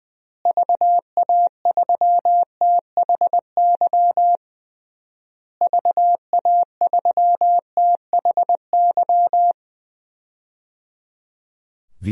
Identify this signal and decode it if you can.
Morse code, heard as a keyed tone, about 21 wpm, 700 Hz. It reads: VA3THY VA3THY
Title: DXCC-100 Call Signs - 6 Characters - Repeated 3x - 20wpm